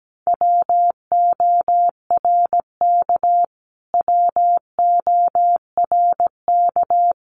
Morse code, heard as a keyed tone, about 17 wpm, 700 Hz. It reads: WORK WORK
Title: Top 200 Words Repeated 17wpm